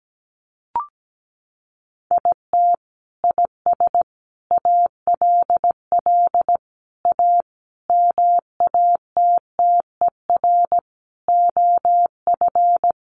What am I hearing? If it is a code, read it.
Morse code, heard as a keyed tone, about 17 wpm, 700 Hz. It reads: IT IS ALL A MATTER OF
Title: Sentences from Top 1300 Words 17wpm